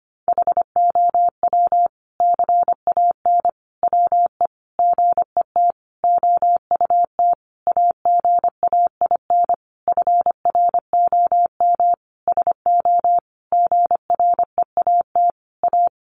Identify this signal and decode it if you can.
Morse code, heard as a keyed tone, about 25 wpm, 700 Hz. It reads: HOW CAN WE GET OUT AGAIN FROM SO GREAT A
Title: Sentences from Top 1200 Words 25wpm